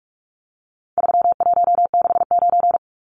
Morse code, heard as a keyed tone, about 44 wpm, 700 Hz. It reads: 3169
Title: Numbers - 4-Digit 45wpm